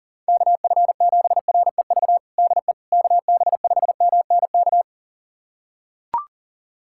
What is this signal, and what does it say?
Morse code, heard as a keyed tone, about 40 wpm, 700 Hz. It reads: KF7REV DE KB5MNK
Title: USA Callsign DE Callsign - 6 Characters - 40wpm